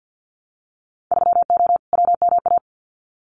Morse code, heard as a keyed tone, about 50 wpm, 700 Hz. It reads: 3O WMA
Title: Winter Field Day Exchanges Encore 50wpm